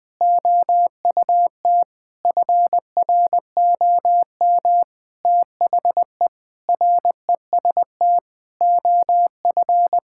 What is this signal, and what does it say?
Morse code, heard as a keyed tone, about 21 wpm, 700 Hz. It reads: OUT FROM THE REST OF
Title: Sentences with 5-Word Phrases - 20wpm